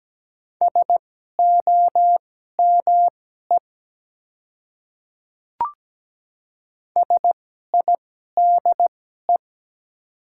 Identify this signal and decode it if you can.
Morse code, heard as a keyed tone, about 17 wpm, spaced as for 12 wpm, 700 Hz. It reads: SOME SIDE
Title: 2x Character Spacing - Top 100 Words - 17wpm